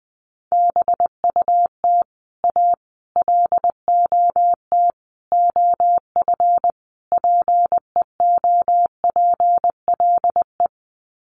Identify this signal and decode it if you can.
Morse code, heard as a keyed tone, about 19 wpm, 700 Hz. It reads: BUT A LOT OF PEOPLE
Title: Sets of 5 Words Encore2 20wpm